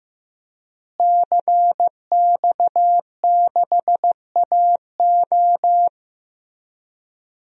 Morse code, heard as a keyed tone, about 15 wpm, 700 Hz. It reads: CX6AO
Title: DXCC-100 Call Signs - 5 Characters - Repeated 3x - 15wpm